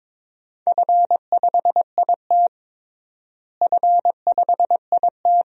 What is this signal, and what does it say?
Morse code, heard as a keyed tone, about 22 wpm, 700 Hz. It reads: F5IT F5IT
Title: DXCC-100 Call Signs - 4 Characters - Repeated 2x - 22wpm